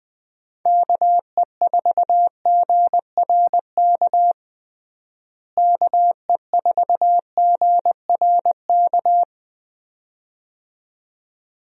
Morse code, heard as a keyed tone, about 21 wpm, 700 Hz. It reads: KE4GRK KE4GRK
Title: USA Call Signs - 6 Characters - Repeated 2x - 20wpm